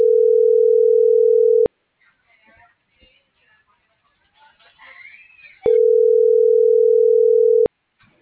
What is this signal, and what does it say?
Ringback tone (North America), 440 + 480 Hz.
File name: CENTRAL SOTOVA / out alejandro.rodriguez - .58072